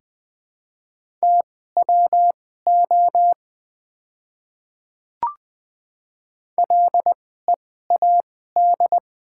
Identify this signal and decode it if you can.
Morse code, heard as a keyed tone, about 20 wpm, spaced as for 14 wpm, 700 Hz. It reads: TWO LEAD
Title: 2x Character Spacing - Top 500 Words - 20wpm